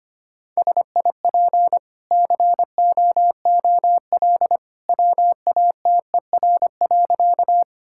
Morse code, heard as a keyed tone, about 25 wpm, 690 Hz. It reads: SIP COOL WATER.
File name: Easy Sentences with Hints - 25wpm